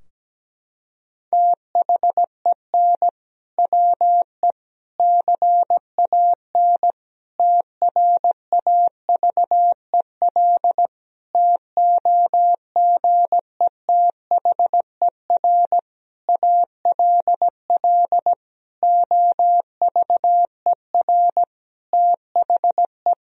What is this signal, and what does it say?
Morse code, heard as a keyed tone, about 17 wpm, 700 Hz. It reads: THEN WE CAN TRAVEL TOGETHER ALL OVER THE
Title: Sentences from Top 1000 Words - Review Part 2 - 17wpm